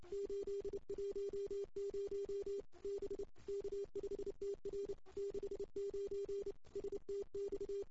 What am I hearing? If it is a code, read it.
Morse code, heard as a keyed tone, about 30 wpm, 400 Hz. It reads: 810 B K5TR 69 STX
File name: CW / 2008-ss-cw-k5tr-op-n5rz -05:00